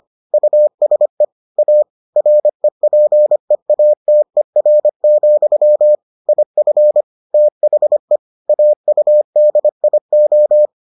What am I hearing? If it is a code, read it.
Morse code, heard as a keyed tone, about 25 wpm, 600 Hz. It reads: USE A REPEATER, IF THE AUDIO